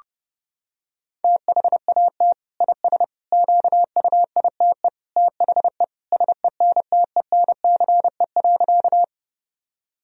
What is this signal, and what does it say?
Morse code, heard as a keyed tone, about 30 wpm, 700 Hz. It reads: THAT IS QUITE THE SENTENCE.
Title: Sentences from Top 1000 Words - Review Part 1 - 30wpm